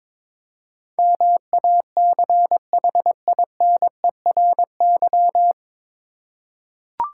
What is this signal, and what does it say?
Morse code, heard as a keyed tone, about 23 wpm, 690 Hz. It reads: MACHINERY
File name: 9 Letter Words - First 3 Letters Given - 22wpm